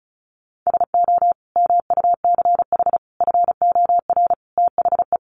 Morse code, heard as a keyed tone, about 34 wpm, 700 Hz. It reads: SO MUCH FOR THE
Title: Sentences from Top 1000 Words 35wpm